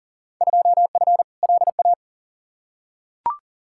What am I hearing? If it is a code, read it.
Morse code, heard as a keyed tone, about 39 wpm, 700 Hz. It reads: 2F LA
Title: Field Day Exchanges - 40wpm